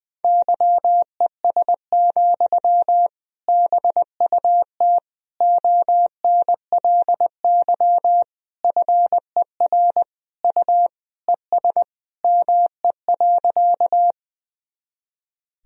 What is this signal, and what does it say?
Morse code, heard as a keyed tone, about 20 wpm, 700 Hz. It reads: YES, BUT ONLY FER U ES ME.
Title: Sentences from Top 500 Words and Abbreviations 20wpm